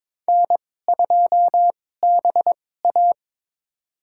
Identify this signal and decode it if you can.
Morse code, heard as a keyed tone, about 21 wpm, spaced as for 16 wpm, 700 Hz. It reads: N2BA
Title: USA Call Signs - 4 Characters - 2x Character Spacing - 22wpm